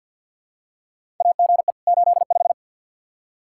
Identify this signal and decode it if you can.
Morse code, heard as a keyed tone, about 47 wpm, 700 Hz. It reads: AGE 85
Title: Practice QSO - Exchange 3 - More Info - 50wpm